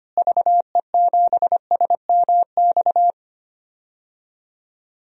Morse code, heard as a keyed tone, about 25 wpm, 700 Hz. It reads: VE7SMX
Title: DXCC-100 Call Signs - 6 Characters - Repeated 2x - 25wpm